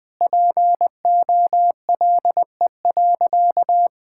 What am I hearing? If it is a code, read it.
Morse code, heard as a keyed tone, about 20 wpm, 700 Hz. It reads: POLE.
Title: Sentences with 4-Word Phrases - Spoken Beginning - 20wpm